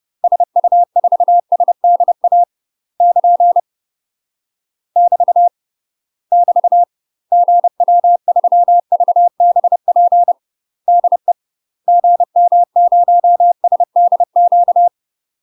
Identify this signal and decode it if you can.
Morse code, heard as a keyed tone, about 30 wpm, 700 Hz. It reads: SU4SDA <KN> <BT> <BT> GW3VBP DE GM0SDQ
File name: UK MASTER QSOs TRAINING / 30WPM / UK MASTER QSOs 0015